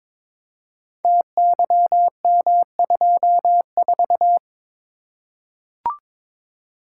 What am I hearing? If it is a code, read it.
Morse code, heard as a keyed tone, about 22 wpm, 700 Hz. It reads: TYM24
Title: Code Groups - 5 Letter:Numbers - 22wpm